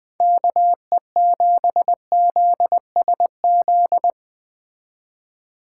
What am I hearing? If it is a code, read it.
Morse code, heard as a keyed tone, about 21 wpm, 700 Hz. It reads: KE7ZSZ
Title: Speed Racing - USA 2x3 Call Signs 17wpm